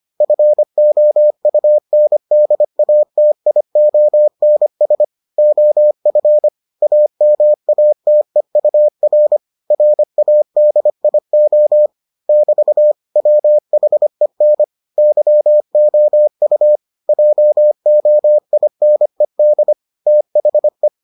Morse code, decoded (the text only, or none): FOUNDATIONS OF AMATEUR RADIO <BT> WHEN YOU JOINED THE